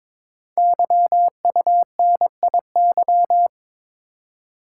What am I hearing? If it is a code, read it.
Morse code, heard as a keyed tone, about 22 wpm, 700 Hz. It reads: YUNIY
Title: Code Groups - 5 Letters - 22wpm